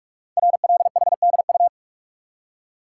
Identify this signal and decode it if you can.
Morse code, heard as a keyed tone, about 44 wpm, 700 Hz. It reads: RLHDU
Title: Code Groups - 5 Letters - 45wpm